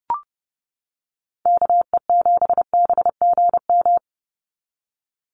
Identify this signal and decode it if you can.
Morse code, heard as a keyed tone, about 30 wpm, 700 Hz. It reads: KE7BGM
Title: NA SOTA Activator Callsigns 30wpm